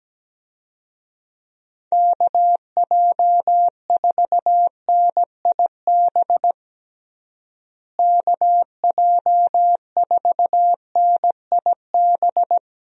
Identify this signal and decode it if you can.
Morse code, heard as a keyed tone, about 17 wpm, 700 Hz. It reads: KJ4NIB KJ4NIB
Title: USA Call Signs - 6 Characters - Repeated 3x - 17wpm